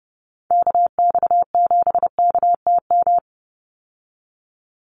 Morse code, heard as a keyed tone, about 30 wpm, 700 Hz. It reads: KX7KTM
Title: USA Call Signs 30wpm